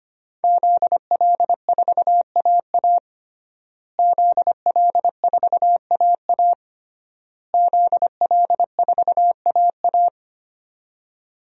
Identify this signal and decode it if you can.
Morse code, heard as a keyed tone, about 25 wpm, 700 Hz. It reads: ZL4AA ZL4AA ZL4AA
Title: DXCC-100 Call Signs - 5 Characters - Repeated 3x - 25wpm